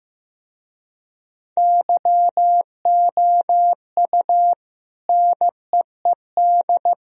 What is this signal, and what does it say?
Morse code, heard as a keyed tone, about 17 wpm, 700 Hz. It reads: YOU NEED
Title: Speed-Racing - Sets of 3 Words 15wpm